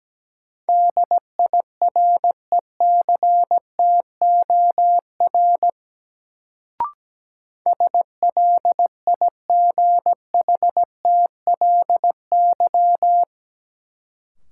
Morse code, heard as a keyed tone, about 17 wpm, 700 Hz. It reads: DIRECTOR SLIGHTLY
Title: Sentences from Top 1300 Words - New Word Drill - 17wpm